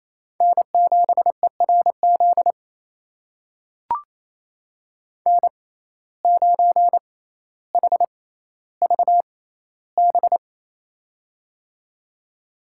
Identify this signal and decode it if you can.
Morse code, heard as a keyed tone, about 28 wpm, spaced as for 14 wpm, 700 Hz. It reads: N7ERZ N9HVB
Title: USA Call Signs - 6x Character Spacing - 28wpm